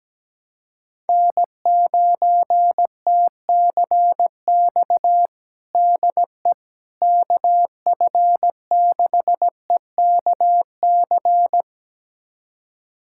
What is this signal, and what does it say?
Morse code, heard as a keyed tone, about 17 wpm, 700 Hz. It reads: N9TCX DE KF6EKC
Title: USA Callsign DE Callsign 17wpm